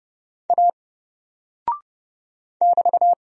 Morse code, heard as a keyed tone, about 30 wpm, 700 Hz. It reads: A <BT>
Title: Single Character (Letter:Number:Punctuation:Prosign) Alt 30wpm